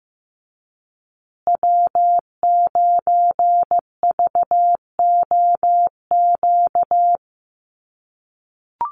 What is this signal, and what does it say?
Morse code, heard as a keyed tone, about 15 wpm, 700 Hz. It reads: W9VOQ
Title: USA Call Signs Repeated 3x 15wpm